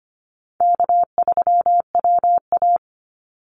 Morse code, heard as a keyed tone, about 25 wpm, 700 Hz. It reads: K3WA
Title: Easy Call Signs - 25wpm